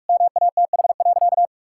Morse code, heard as a keyed tone, about 46 wpm, 700 Hz. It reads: MATH.
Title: Sentences from Top 1000 Words - Review Part 6 - 45wpm